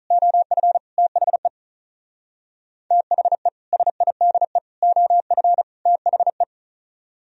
Morse code, heard as a keyed tone, about 43 wpm, 700 Hz. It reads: OF THE THE SIDE OF THE
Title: Speed-Racing - Sets of 4 Words 35wpm